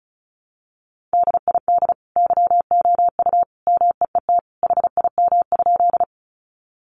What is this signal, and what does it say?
Morse code, heard as a keyed tone, about 34 wpm, 700 Hz. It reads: DID YOU MEET HIM?
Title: Sentences from Top 1100 Words 35wpm